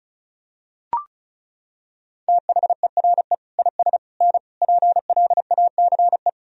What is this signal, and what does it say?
Morse code, heard as a keyed tone, about 34 wpm, 700 Hz. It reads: THERE IS N PLACE
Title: Sentences from Top 200 Words and Abbreviations 35wpm